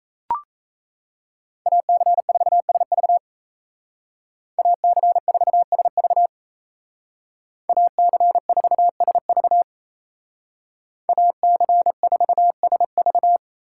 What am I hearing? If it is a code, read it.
Morse code, heard as a keyed tone, about 37 wpm, 700 Hz. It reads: AC4SV AC4SV AC4SV AC4SV